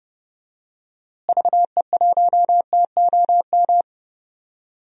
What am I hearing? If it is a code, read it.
Morse code, heard as a keyed tone, about 30 wpm, 700 Hz. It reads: VE1TOM
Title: DXCC-100 Call Signs - 6 Characters - Repeated 3x - 30wpm